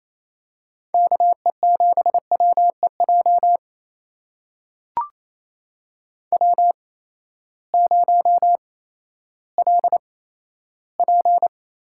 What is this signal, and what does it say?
Morse code, heard as a keyed tone, about 28 wpm, spaced as for 14 wpm, 700 Hz. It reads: KE7WEJ W0LP
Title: USA Call Signs - 8x Character Spacing - 28wpm